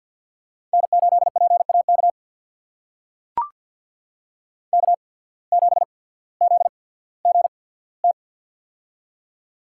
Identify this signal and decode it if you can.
Morse code, heard as a keyed tone, about 53 wpm, spaced as for 22 wpm, 700 Hz. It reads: N8PAK K7ZGT
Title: USA Call Signs - 5 Characters - 8x Character Spacing - 50wpm